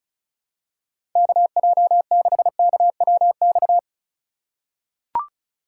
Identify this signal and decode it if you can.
Morse code, heard as a keyed tone, about 36 wpm, 700 Hz. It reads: KJ6KWX
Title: USA Call Signs - 6 Characters - 2x Character Spacing - 35wpm